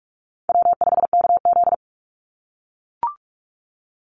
Morse code, heard as a keyed tone, about 44 wpm, 690 Hz. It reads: W5KZ
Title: USA Call Signs 45wpm